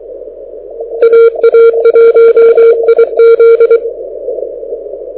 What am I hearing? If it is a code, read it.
Morse code, heard as a keyed tone, about 23 wpm, 2300 Hz. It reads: AA1IZ
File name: audio zawody / IARU 2015 CW SN7Q